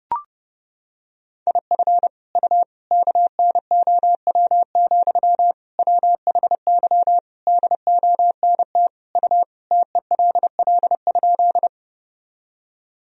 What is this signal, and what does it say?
Morse code, heard as a keyed tone, about 30 wpm, 700 Hz. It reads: IF U KNOW, WHY DONT U TELL?
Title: Sentences from Top 300 Words and Abbreviations 30wpm